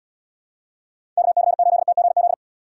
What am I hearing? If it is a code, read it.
Morse code, heard as a keyed tone, about 76 wpm, 700 Hz. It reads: KC8TGZ